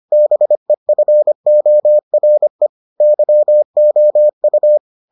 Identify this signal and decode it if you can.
Morse code, heard as a keyed tone, about 25 wpm, 600 Hz. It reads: BEFORE YOU